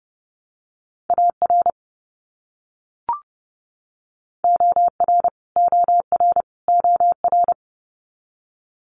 Morse code, heard as a keyed tone, about 29 wpm, 700 Hz. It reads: AR OR OR OR
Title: US States Repeated 3x 30wpm